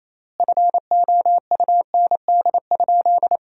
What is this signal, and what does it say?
Morse code, heard as a keyed tone, about 28 wpm, 700 Hz. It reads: FOUND?
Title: 3x Word-Spacing - Sentences from Top 300 Words - 28wpm